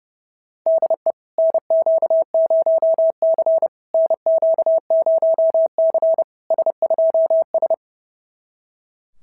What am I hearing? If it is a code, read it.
Morse code, heard as a keyed tone, about 30 wpm, 690 Hz. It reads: DE NQ0C NQ0C S2S
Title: SOTA QSO Practice 30wpm